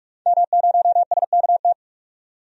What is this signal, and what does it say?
Morse code, heard as a keyed tone, about 44 wpm, 690 Hz. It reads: M0SKT
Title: DXCC-100 Call Signs - 5 Characters - 45wpm